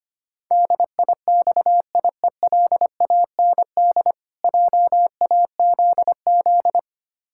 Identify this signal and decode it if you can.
Morse code, heard as a keyed tone, about 25 wpm, 700 Hz. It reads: DIXIELAND JAZZ
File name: practice / 25wpm / Difficult Words - Speed-Racing with Hints - 25wpm